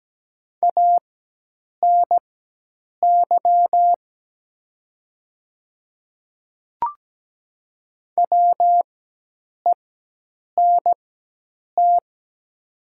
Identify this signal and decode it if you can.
Morse code, heard as a keyed tone, about 17 wpm, spaced as for 8 wpm, 700 Hz. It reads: ANY WENT
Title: 4x Character Spacing - Top 300 Words - 17wpm